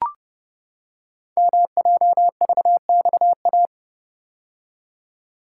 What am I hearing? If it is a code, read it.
Morse code, heard as a keyed tone, about 29 wpm, 700 Hz. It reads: MJVXA
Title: Code Groups - 5 Letters - 30wpm